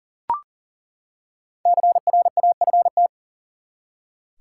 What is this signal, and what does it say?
Morse code, heard as a keyed tone, about 38 wpm, 700 Hz. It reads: CRAFT
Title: Five Letter Words - 40wpm